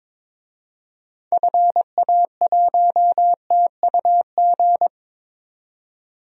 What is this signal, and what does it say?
Morse code, heard as a keyed tone, about 22 wpm, 700 Hz. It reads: FA1TUG
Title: DXCC-100 Call Signs - 6 Characters - 22wpm